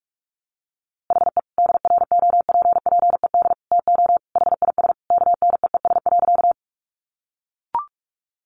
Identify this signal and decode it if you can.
Morse code, heard as a keyed tone, about 45 wpm, 700 Hz. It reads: HE DROPPED TO HIS KNEES.
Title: A Sound Of Thunder 45wpm